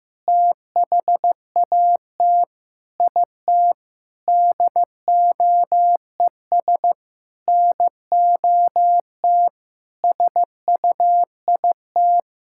Morse code, heard as a keyed tone, about 16 wpm, 700 Hz. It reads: THAT IT DOES NOT SUIT
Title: Sentences from Top 1600 Words 15wpm